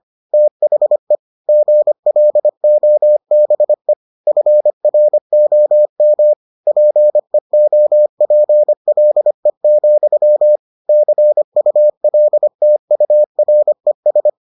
Morse code, heard as a keyed tone, about 25 wpm, 600 Hz. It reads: THE GLOBE FROM PEOPLE, CULTURES